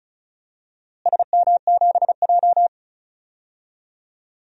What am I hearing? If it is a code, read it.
Morse code, heard as a keyed tone, about 34 wpm, 700 Hz. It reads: SM7J
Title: DXCC-100 Call Signs - 4 Characters - Repeated 2x - 35wpm